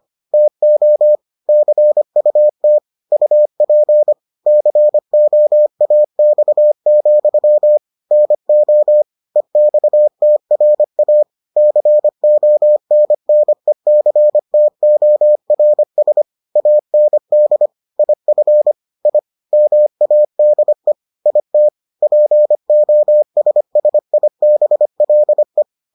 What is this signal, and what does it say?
Morse code, heard as a keyed tone, about 25 wpm, 600 Hz. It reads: TO CUT UP COAX, NO EXTRA CONNECTORS AND IF I MADE IT POSSIBLE